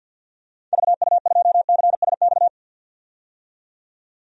Morse code, heard as a keyed tone, about 45 wpm, 700 Hz. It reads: VU2CSX
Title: DXCC-100 Call Signs - 6 Characters - 50wpm